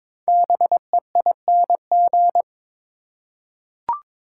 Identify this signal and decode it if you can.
Morse code, heard as a keyed tone, about 22 wpm, 700 Hz. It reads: BEING
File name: 5 Letter Words - First 3 Letters Given - 22wpm